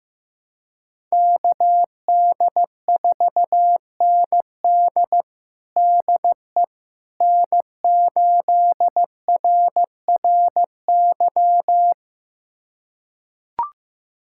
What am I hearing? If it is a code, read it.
Morse code, heard as a keyed tone, about 15 wpm, 700 Hz. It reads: KD4ND DE N8RRY
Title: USA Callsign DE Callsign - 5 Characters - 15wpm